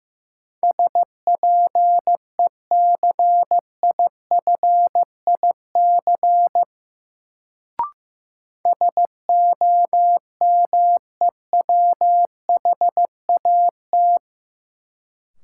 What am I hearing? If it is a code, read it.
Morse code, heard as a keyed tone, about 15 wpm, 700 Hz. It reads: SPECIFIC SOMEWHAT
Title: Sentences from Top 1300 Words - New Word Drill Alt - 15wpm